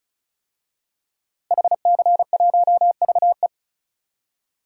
Morse code, heard as a keyed tone, about 36 wpm, 700 Hz. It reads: HC1VE
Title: DXCC-100 Call Signs - 5 Characters - Repeated 2x - 35wpm